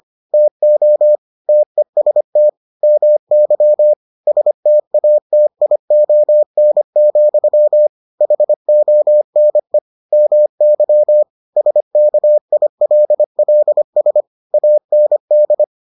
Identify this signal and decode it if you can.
Morse code, heard as a keyed tone, about 25 wpm, 600 Hz. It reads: TO TEST MY STATION, HONE MY SKILLS AND